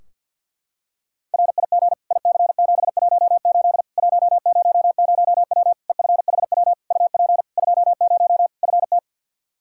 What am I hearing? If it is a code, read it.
Morse code, heard as a keyed tone, about 52 wpm, 700 Hz. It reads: RIG IC718 100W EFHW UP 20 FT
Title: Practice QSO - Exchange 2 - Info - 50wpm